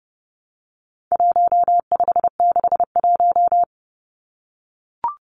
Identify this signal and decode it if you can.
Morse code, heard as a keyed tone, about 30 wpm, 700 Hz. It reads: 1561
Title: Numbers - 4-Digit 30wpm